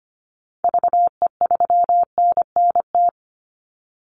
Morse code, heard as a keyed tone, about 25 wpm, 700 Hz. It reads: VE3NNT
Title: Call Signs - 25wpm